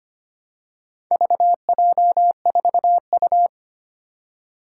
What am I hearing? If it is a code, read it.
Morse code, heard as a keyed tone, about 25 wpm, 700 Hz. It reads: VJ4U
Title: Four Challenging Letter-Number Combinations 25wpm